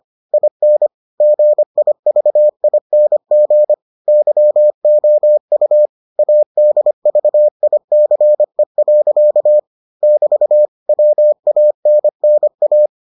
Morse code, heard as a keyed tone, about 25 wpm, 600 Hz. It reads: IN GIVING YOU ADVICE. <BT> WANNA